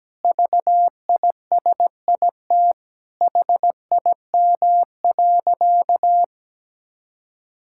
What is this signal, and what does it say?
Morse code, heard as a keyed tone, about 17 wpm, 700 Hz. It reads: VISIT HIM.
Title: Sentences from Top 1000 Words - Review Part 6 - 17wpm